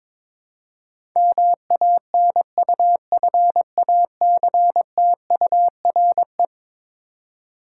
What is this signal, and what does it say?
Morse code, heard as a keyed tone, about 21 wpm, 700 Hz. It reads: MANUFACTURE
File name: practice / 22wpm / Eleven Letter Words - 22wpm